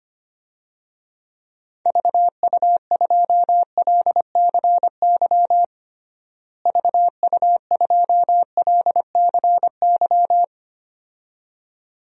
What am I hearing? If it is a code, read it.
Morse code, heard as a keyed tone, about 25 wpm, 700 Hz. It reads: VU2LCY VU2LCY
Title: DXCC-100 Call Signs - 6 Characters - Repeated 2x - 25wpm